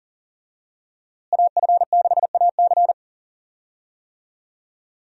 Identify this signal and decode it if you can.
Morse code, heard as a keyed tone, about 38 wpm, 700 Hz. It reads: AF6AC
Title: USA Call Signs - 5 Characters - Repeated 3x - 40wpm